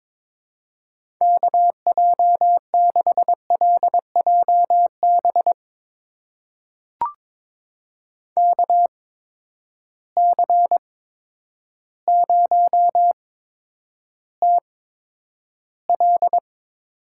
Morse code, heard as a keyed tone, about 22 wpm, 700 Hz. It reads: K J 6 L J B K C 0 T L
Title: USA Call Signs - 6 Characters - 8x Character Spacing - 22wpm